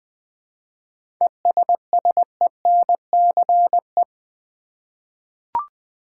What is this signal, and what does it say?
Morse code, heard as a keyed tone, about 20 wpm, 700 Hz. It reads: ESSENCE
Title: 7 Letter Words - First 3 Letters Given - 20wpm